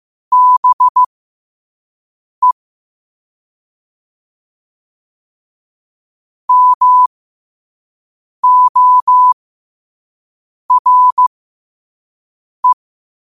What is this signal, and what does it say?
Morse code, heard as a keyed tone, about 15 wpm, spaced as for 5 wpm, 1000 Hz. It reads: BE MORE
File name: 5wpm